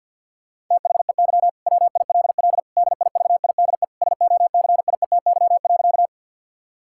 Morse code, heard as a keyed tone, about 50 wpm, 700 Hz. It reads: THEY WILL DIVIDE SOCIETY.
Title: Sentences from Top 1600 Words 50wpm